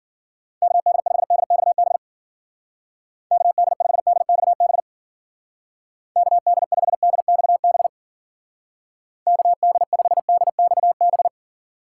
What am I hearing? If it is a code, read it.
Morse code, heard as a keyed tone, about 48 wpm, 700 Hz. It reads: KD5DXB KD5DXB KD5DXB KD5DXB
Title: Speed Racing - USA 2x3 Call Signs 40wpm